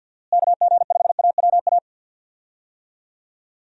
Morse code, heard as a keyed tone, about 51 wpm, 700 Hz. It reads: KG5AWA
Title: USA Call Signs - 6 Characters - Repeated 2x - 50wpm